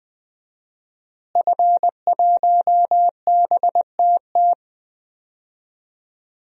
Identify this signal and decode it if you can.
Morse code, heard as a keyed tone, about 20 wpm, 700 Hz. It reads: F1BTT
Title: DXCC-100 Call Signs - 5 Characters - Repeated 3x - 20wpm